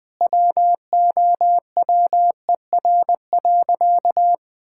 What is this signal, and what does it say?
Morse code, heard as a keyed tone, about 20 wpm, 700 Hz. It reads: WOWER.
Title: Pangram Challenge v2 - 20wpm